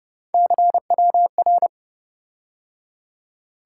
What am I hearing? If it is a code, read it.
Morse code, heard as a keyed tone, about 27 wpm, 700 Hz. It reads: CWR